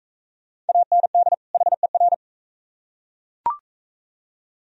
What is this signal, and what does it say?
Morse code, heard as a keyed tone, about 39 wpm, 700 Hz. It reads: AND HER